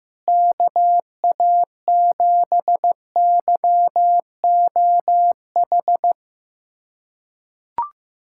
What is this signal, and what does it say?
Morse code, heard as a keyed tone, about 16 wpm, 700 Hz. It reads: KA7YOH
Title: USA Callsign DE Callsign - 6 Characters - 15wpm